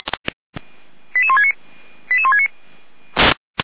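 Five-tone selective call (ZVEI), heard as quick runs of five tones, tones of about 70 ms. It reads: 88169 88169
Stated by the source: ZVEI five-tone selective call 88133 received on an SDR.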